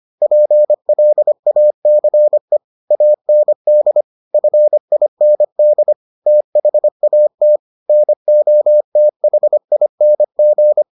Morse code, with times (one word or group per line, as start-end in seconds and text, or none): 0.2-2.6 PLACE
2.9-4.0 AND
4.3-5.9 FIND
6.3-7.6 THAT
7.9-10.8 NOTHING